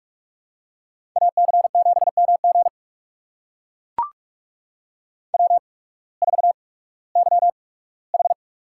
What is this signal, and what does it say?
Morse code, heard as a keyed tone, about 44 wpm, spaced as for 21 wpm, 700 Hz. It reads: AC7MG W4YH
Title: USA Call Signs - 8x Character Spacing - 45wpm